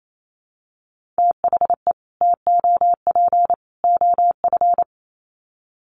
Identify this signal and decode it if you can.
Morse code, heard as a keyed tone, about 28 wpm, 700 Hz. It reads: THE TOP OF
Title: Sets of 3 Words 28wpm